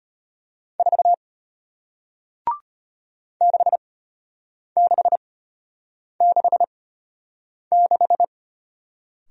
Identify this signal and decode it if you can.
Morse code, heard as a keyed tone, about 34 wpm, spaced as for 8 wpm, 700 Hz. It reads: 4 6666